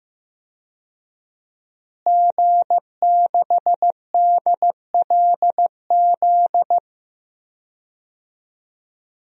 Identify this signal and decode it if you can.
Morse code, heard as a keyed tone, about 15 wpm, 700 Hz. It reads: G6DLZ
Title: DXCC-100 Call Signs - 5 Characters - Repeated 3x - 15wpm